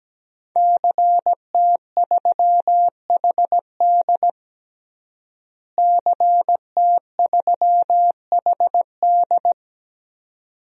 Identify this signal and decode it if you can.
Morse code, heard as a keyed tone, about 17 wpm, 700 Hz. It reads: CT3HD CT3HD
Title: DXCC-100 Call Signs - 5 Characters - Repeated 2x - 17wpm